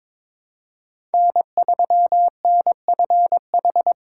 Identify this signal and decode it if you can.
Morse code, heard as a keyed tone, about 22 wpm, 700 Hz. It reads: N3NFH
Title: USA Call Signs - 4x Character Spacing - 22wpm